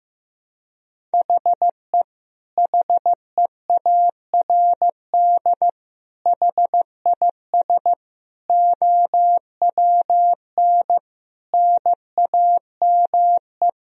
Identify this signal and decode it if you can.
Morse code, heard as a keyed tone, about 15 wpm, 700 Hz. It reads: HE HEARD HIS OWN NAME
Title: Sentences from Top 1000 Words - Review Part 2 - 15wpm